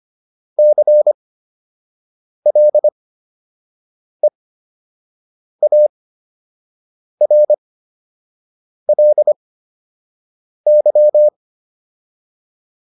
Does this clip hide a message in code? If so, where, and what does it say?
Yes: Morse code from 0.6 to 11.3 s, heard as a keyed tone, about 25 wpm, spaced as for 6 wpm, 600 Hz. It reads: CLEARLY